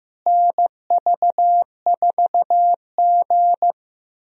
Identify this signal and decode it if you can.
Morse code, heard as a keyed tone, about 15 wpm, 700 Hz. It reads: NV4G
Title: USA Call Signs - 4 Characters - Repeated 3x - 15wpm